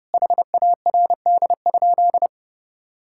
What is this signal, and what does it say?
Morse code, heard as a keyed tone, about 32 wpm, 700 Hz. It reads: HARD?
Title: Sentences from Top 1000 Words - Review Part 5 - 30wpm